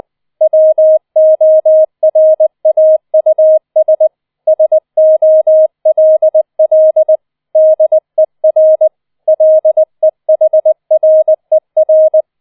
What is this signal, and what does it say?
Morse code, heard as a keyed tone, about 20 wpm, 630 Hz. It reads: WORAUS SOLL DER LEHRER